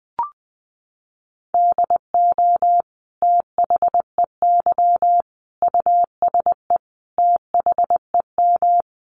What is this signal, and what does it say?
Morse code, heard as a keyed tone, about 20 wpm, 700 Hz. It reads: DO THEY USE THEM
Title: Sentences from Top 1300 Words 20wpm